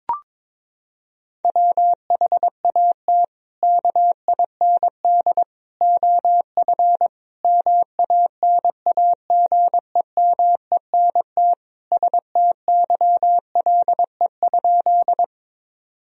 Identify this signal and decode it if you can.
Morse code, heard as a keyed tone, about 22 wpm, 700 Hz. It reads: WHAT KIND OF MANAGEMENT STYLE?
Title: Sentences from Top 1000 Words - Review Part 4 - 22wpm